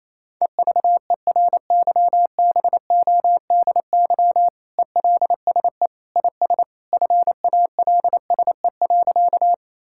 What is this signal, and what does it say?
Morse code, heard as a keyed tone, about 28 wpm, 700 Hz. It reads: EVERYBODY ELSE IS FALSE.
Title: Sentences from Top 1600 Words 28wpm